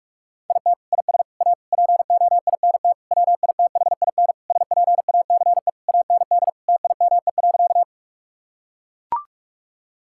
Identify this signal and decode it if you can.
Morse code, heard as a keyed tone, about 41 wpm, 700 Hz. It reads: IT IS A POINT WITHIN SPACE AND TIME.
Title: Sentences from Top 1000 Words - Review Part 5 - 45wpm